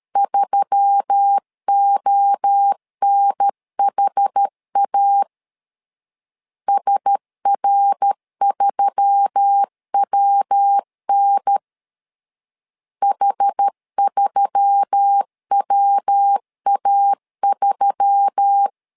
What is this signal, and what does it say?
Morse code, heard as a keyed tone, about 13 wpm, 800 Hz. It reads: 3ONHA SR3WN H3WA3
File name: LEZIONI / 4 Q1S3 12wpm 2023